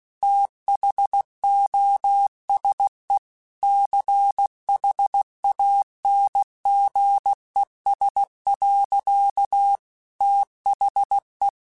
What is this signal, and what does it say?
Morse code, heard as a keyed tone, about 16 wpm, 800 Hz. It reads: THOSE CHANGES. THE
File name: Technical Subjects / Technical Subjects-mp3s / XC @15